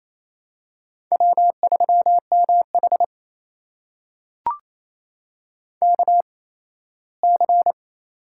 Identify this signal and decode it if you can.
Morse code, heard as a keyed tone, about 28 wpm, spaced as for 17 wpm, 700 Hz. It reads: W3MH KC
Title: USA Call Signs - 8x Character Spacing - 28wpm